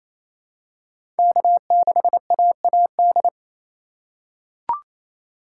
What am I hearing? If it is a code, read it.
Morse code, heard as a keyed tone, about 28 wpm, 700 Hz. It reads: K6AAD
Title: USA Call Signs - 5 Characters - 8x Character Spacing - 28wpm